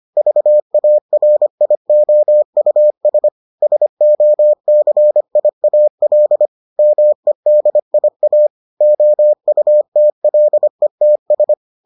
Morse code, heard as a keyed tone, about 25 wpm, 600 Hz. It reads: VARIOUS SOCIAL MEDIA OUTLETS